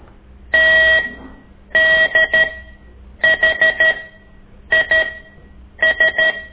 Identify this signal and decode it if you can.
Morse code, heard as a keyed tone, about 13 wpm, spaced as for 8 wpm, 1900 Hz. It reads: TDHIS
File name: ik1weg2 / CORSI CW